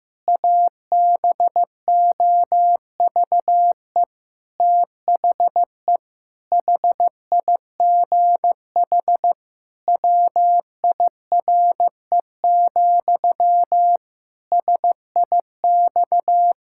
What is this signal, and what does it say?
Morse code, heard as a keyed tone, about 15 wpm, 700 Hz. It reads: ABOVE THE HIGH WIRE, SIX
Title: Pangram Challenge v2 - 15wpm